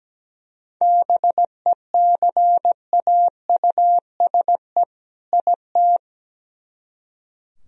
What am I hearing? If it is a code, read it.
Morse code, heard as a keyed tone, about 17 wpm, 700 Hz. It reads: BECAUSE IT
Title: Speed-Racing - Sets of 2 Words 17wpm